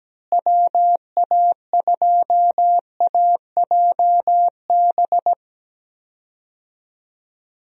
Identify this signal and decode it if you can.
Morse code, heard as a keyed tone, about 17 wpm, 700 Hz. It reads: WA2AJB
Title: USA Call Signs - 6 Characters - Repeated 3x - 17wpm